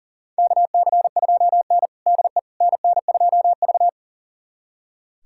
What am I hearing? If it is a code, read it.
Morse code, heard as a keyed tone, about 40 wpm, 700 Hz. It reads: KC2N DE NN2V
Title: USA Callsign DE Callsign - 4 Characters - 40wpm